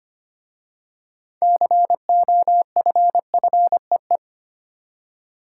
Morse code, heard as a keyed tone, about 25 wpm, 700 Hz. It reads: COFFEE
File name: practice / 25wpm / Easy Words with Hints - 25wpm